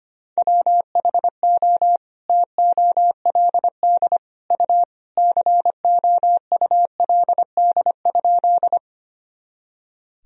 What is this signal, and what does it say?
Morse code, heard as a keyed tone, about 25 wpm, 700 Hz. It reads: WHO TOLD U COULD?
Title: Sentences from Top 400 Words and Abbreviations 25wpm